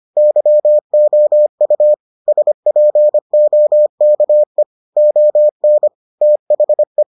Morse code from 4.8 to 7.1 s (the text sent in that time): ON THE